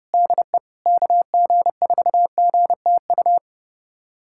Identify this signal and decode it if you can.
Morse code, heard as a keyed tone, about 32 wpm, 700 Hz. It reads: DE KG4GTU
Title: USA Callsign DE Callsign - 6 Characters - 30wpm